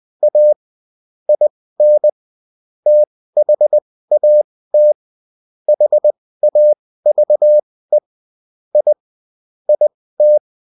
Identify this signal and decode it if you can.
Morse code, heard as a keyed tone, about 20 wpm, spaced as for 15 wpm, 600 Hz. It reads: A IN THAT HAVE I IT
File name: Top-100-Words-15